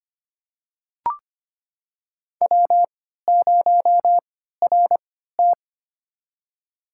Morse code, heard as a keyed tone, about 25 wpm, spaced as for 14 wpm, 700 Hz. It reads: W0RT
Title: USA Call Signs - 4 Characters - 3x Character Spacing - 25wpm